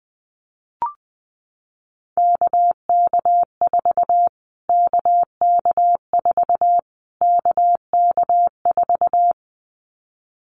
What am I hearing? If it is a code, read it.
Morse code, heard as a keyed tone, about 20 wpm, 700 Hz. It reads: KK4 KK4 KK4
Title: Callsign Prefixes for Top 25 DXCC - Repeated 3x - 20wpm